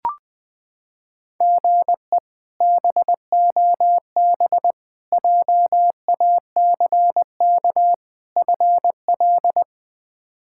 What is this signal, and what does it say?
Morse code, heard as a keyed tone, about 20 wpm, 700 Hz. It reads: GE BOB JACK FL
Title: K1USN SST QSO Practice - 20wpm